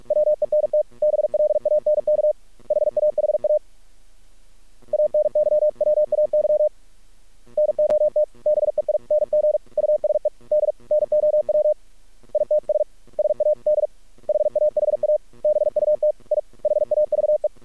Morse code, heard as a keyed tone, about 48 wpm, 610 Hz. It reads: WENT BLANK HAHA ANYWAY NOT BEING USED NOW ITS SAD HAHA BUT I HAVE
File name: archived / CRSnet